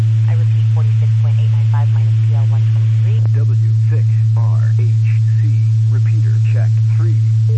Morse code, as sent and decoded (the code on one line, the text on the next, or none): none